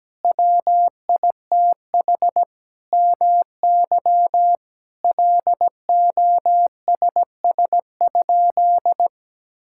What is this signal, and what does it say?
Morse code, heard as a keyed tone, about 17 wpm, 700 Hz. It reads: WITH MY LOSS?
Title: Sentences from Top 1600 Words 17wpm